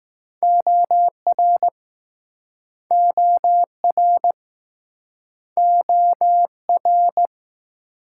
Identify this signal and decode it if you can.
Morse code, heard as a keyed tone, about 20 wpm, 700 Hz. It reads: OR OR OR